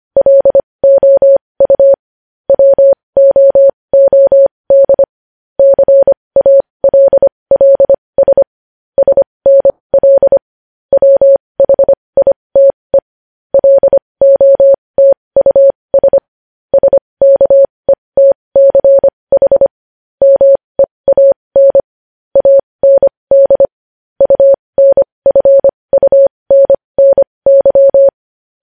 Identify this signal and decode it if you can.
Morse code, heard as a keyed tone, about 25 wpm, spaced as for 20 wpm, 550 Hz. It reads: LOU WOOD CALLS SNL WHITE LOTUS SKETCH MEAN AND UNFUNNY
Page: news-headlines-in-morse-code-at-20-wpm